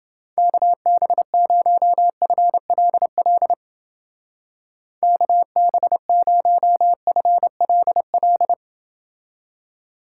Morse code, heard as a keyed tone, about 28 wpm, 700 Hz. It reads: KB0FLL KB0FLL